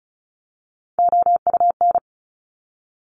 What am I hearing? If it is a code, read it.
Morse code, heard as a keyed tone, about 36 wpm, 700 Hz. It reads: OUN
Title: Top Three Letter Combinations in Words 35wpm